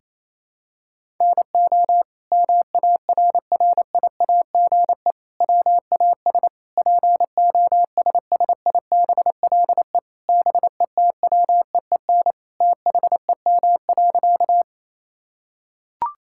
Morse code, heard as a keyed tone, about 28 wpm, 700 Hz. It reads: NO MARRIAGE WAS POSSIBLE BETWEEN THEM.
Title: Sentences from Top 1000 Words - Review Part 5 - 28wpm